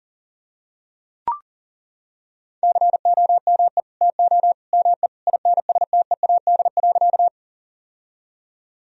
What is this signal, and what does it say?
Morse code, heard as a keyed tone, about 40 wpm, 700 Hz. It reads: COME TO ME INSTEAD.
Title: Sentences from Top 1000 Words - Review Part 4 - 40wpm